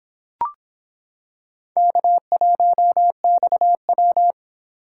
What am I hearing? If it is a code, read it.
Morse code, heard as a keyed tone, about 25 wpm, 700 Hz. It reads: K1XW